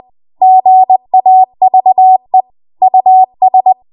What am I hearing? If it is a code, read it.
Morse code, heard as a keyed tone, about 20 wpm, 750 Hz. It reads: GAVE US
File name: Morse / 20WPM